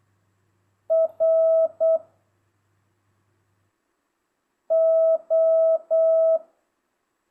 Morse code, heard as a keyed tone, about 8 wpm, 660 Hz. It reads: R O
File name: jh8cbh / 2025cw